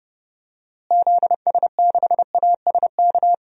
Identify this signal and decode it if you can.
Morse code, heard as a keyed tone, about 32 wpm, 700 Hz. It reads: ZS6ASK
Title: DXCC-100 Call Signs - 6 Characters - Repeated 3x - 30wpm